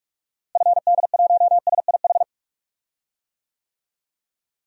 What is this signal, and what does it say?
Morse code, heard as a keyed tone, about 42 wpm, 700 Hz. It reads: FD1SIH
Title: DXCC-100 Call Signs - 6 Characters - Repeated 3x - 45wpm